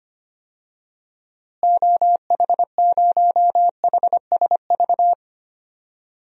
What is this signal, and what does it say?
Morse code, heard as a keyed tone, about 25 wpm, 700 Hz. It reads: OH0HSV
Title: DXCC-100 Call Signs - 6 Characters - Repeated 2x - 25wpm